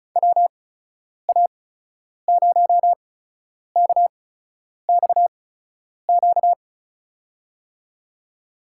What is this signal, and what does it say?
Morse code, heard as a keyed tone, about 36 wpm, spaced as for 10 wpm, 700 Hz. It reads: WA0KXQ
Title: USA Call Signs - 6 Characters - 8x Character Spacing - 35wpm